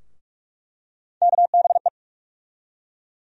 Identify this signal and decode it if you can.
Morse code, heard as a keyed tone, about 42 wpm, 700 Hz. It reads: KBE
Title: USA Call Sign Postfixes - 3 Letters 45wpm